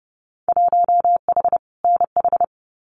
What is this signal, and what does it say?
Morse code, heard as a keyed tone, about 30 wpm, 700 Hz. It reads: 1H NH
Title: Winter Field Day Exchanges 30wpm